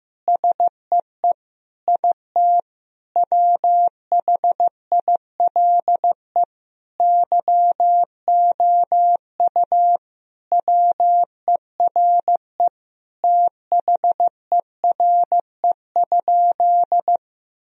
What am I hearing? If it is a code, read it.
Morse code, heard as a keyed tone, about 15 wpm, 700 Hz. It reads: SEE IT WHILE YOU WERE THERE?
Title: Sentences from Top 1000 Words - Review Part 1 - 15wpm